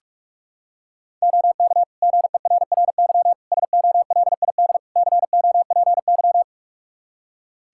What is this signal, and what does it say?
Morse code, heard as a keyed tone, about 46 wpm, 700 Hz. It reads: OK GERRY SOLID COPY
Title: Practice QSO - Exchange 3 - More Info - 45wpm